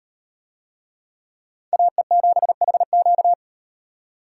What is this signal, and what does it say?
Morse code, heard as a keyed tone, about 38 wpm, 700 Hz. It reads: AE7HQ